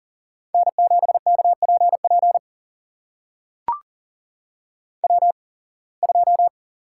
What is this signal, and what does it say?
Morse code, heard as a keyed tone, about 40 wpm, spaced as for 26 wpm, 700 Hz. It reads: N7KPP W2
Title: USA Call Signs - 5 Characters - 8x Character Spacing - 40wpm